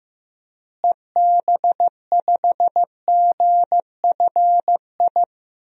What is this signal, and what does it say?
Morse code, heard as a keyed tone, about 15 wpm, 700 Hz. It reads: EB5GFI
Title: DXCC-100 Call Signs - 6 Characters - Repeated 2x - 15wpm